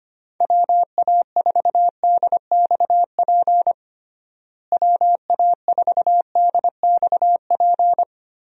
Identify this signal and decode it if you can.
Morse code, heard as a keyed tone, about 25 wpm, 700 Hz. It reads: WA4DXP WA4DXP
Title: USA Call Signs - 6 Characters - Repeated 3x - 25wpm